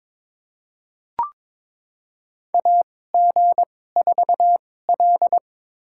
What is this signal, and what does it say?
Morse code, heard as a keyed tone, about 22 wpm, spaced as for 16 wpm, 700 Hz. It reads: AG4L
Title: USA Call Signs - 4 Characters - 2x Character Spacing - 22wpm